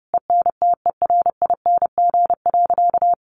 Morse code, heard as a keyed tone, about 30 wpm, 700 Hz. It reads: ENTERING.
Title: A Sound Of Thunder 30wpm